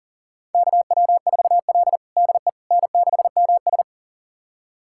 Morse code, heard as a keyed tone, about 40 wpm, 700 Hz. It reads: KW4L DE N6MS
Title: USA Callsign DE Callsign - 4 Characters - 40wpm